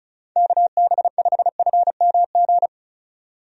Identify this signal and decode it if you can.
Morse code, heard as a keyed tone, about 36 wpm, 700 Hz. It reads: KB5FMG
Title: USA Call Signs - 6 Characters - 35wpm